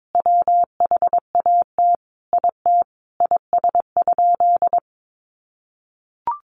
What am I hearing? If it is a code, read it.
Morse code, heard as a keyed tone, about 22 wpm, 700 Hz. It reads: WHAT IT IS?
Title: Sentences from Top 100 Words 22wpm